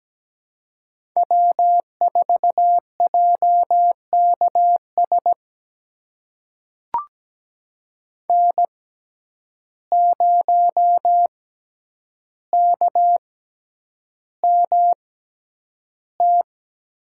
Morse code, heard as a keyed tone, about 17 wpm, spaced as for 9 wpm, 700 Hz. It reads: W4JKS N0KMT
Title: USA Call Signs - 5 Characters - 6x Character Spacing - 17wpm